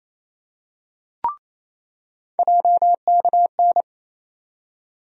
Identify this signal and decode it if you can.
Morse code, heard as a keyed tone, about 28 wpm, 700 Hz. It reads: JKN
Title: USA Call Sign Postfixes - 3 Letters 28wpm